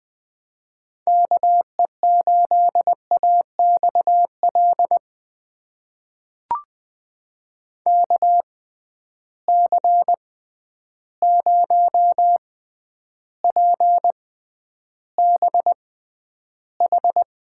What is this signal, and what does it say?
Morse code, heard as a keyed tone, about 21 wpm, spaced as for 10 wpm, 700 Hz. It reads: KE8AXL KC0PBH
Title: USA Call Signs - 6 Characters - 6x Character Spacing - 20wpm